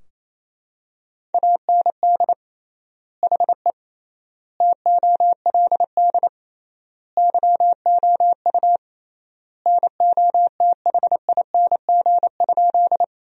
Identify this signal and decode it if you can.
Morse code, heard as a keyed tone, about 27 wpm, 700 Hz. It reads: AND HE TOLD YOU NOTHING?
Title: 3x Word-Spacing - Sentences from Top 500 Words - 28wpm